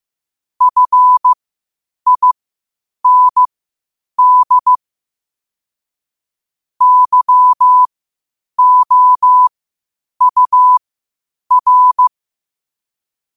Morse code, heard as a keyed tone, about 15 wpm, spaced as for 8 wpm, 1000 Hz. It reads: FIND YOUR
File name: qotdfiles / 8wpm